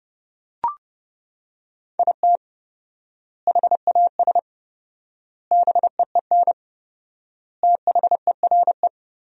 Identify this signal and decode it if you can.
Morse code, heard as a keyed tone, about 32 wpm, 700 Hz. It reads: IT HAS BEEN THERE
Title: 4x Word-Spacing - Sentences from Top 400 Words - 30wpm